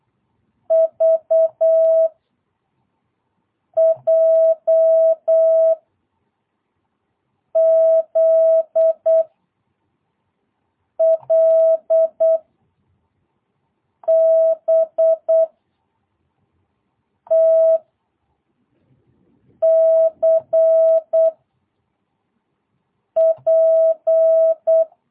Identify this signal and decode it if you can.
Morse code, heard as a keyed tone, about 8 wpm, 660 Hz. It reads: V J Z L B T C P
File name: jh8cbh / 2022cw